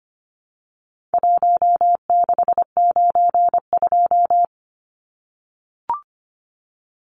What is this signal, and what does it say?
Morse code, heard as a keyed tone, about 25 wpm, 700 Hz. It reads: 1692
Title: Numbers - 4-Digit 25wpm